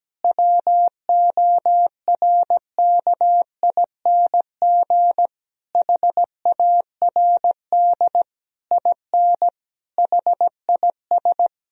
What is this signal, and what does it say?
Morse code, heard as a keyed tone, about 17 wpm, 700 Hz. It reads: WORKING HARD IN HIS
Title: Sentences from Top 1000 Words - Review Part 4 - 17wpm